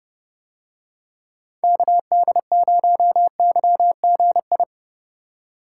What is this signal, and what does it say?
Morse code, heard as a keyed tone, about 29 wpm, 700 Hz. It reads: KD0YGI